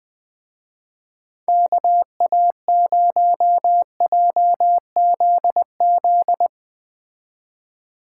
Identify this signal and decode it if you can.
Morse code, heard as a keyed tone, about 20 wpm, 700 Hz. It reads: KA0JZZ
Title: Speed Racing - USA Call Signs 17wpm